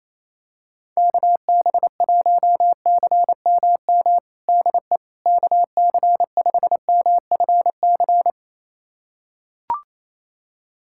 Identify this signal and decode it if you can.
Morse code, heard as a keyed tone, about 27 wpm, 700 Hz. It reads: KB1CMM DE KC5MFC
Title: USA Callsign DE Callsign - 6 Characters - 28wpm